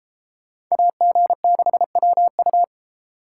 Morse code, heard as a keyed tone, about 36 wpm, 700 Hz. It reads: AG6WU